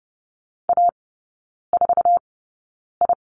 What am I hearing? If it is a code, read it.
Morse code, heard as a keyed tone, about 30 wpm, spaced as for 9 wpm, 700 Hz. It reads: A4I
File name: Lesson 18e - Letter m - Three Characters 30wpm